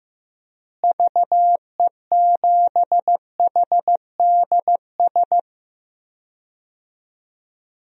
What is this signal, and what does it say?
Morse code, heard as a keyed tone, about 15 wpm, 700 Hz. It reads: VE7HDS
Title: DXCC-100 Call Signs - 6 Characters - Repeated 2x - 15wpm